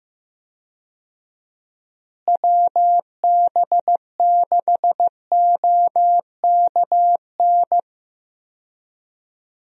Morse code, heard as a keyed tone, about 15 wpm, 700 Hz. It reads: WB6OKN
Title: USA Call Signs - 6 Characters - Repeated 3x - 15wpm